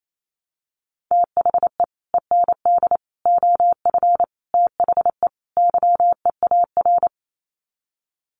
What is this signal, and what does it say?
Morse code, heard as a keyed tone, about 28 wpm, 700 Hz. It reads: THE END OF THE YEAR
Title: Sets of 5 Words 28wpm